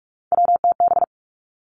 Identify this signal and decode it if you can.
Morse code, heard as a keyed tone, about 46 wpm, 700 Hz. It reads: WTB
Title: Three Challenging Letter-Number Combinations 45wpm